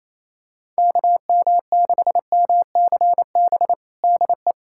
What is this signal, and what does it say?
Morse code, heard as a keyed tone, about 28 wpm, 700 Hz. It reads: KM6MCB DE
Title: USA Callsign DE Callsign 28wpm